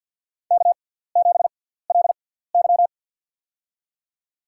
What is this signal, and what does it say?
Morse code, heard as a keyed tone, about 53 wpm, spaced as for 17 wpm, 700 Hz. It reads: K7LY
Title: USA Call Signs - 4 Characters - 6x Character Spacing - 50wpm